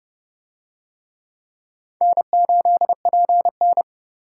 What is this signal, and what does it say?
Morse code, heard as a keyed tone, about 30 wpm, 700 Hz. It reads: N8PN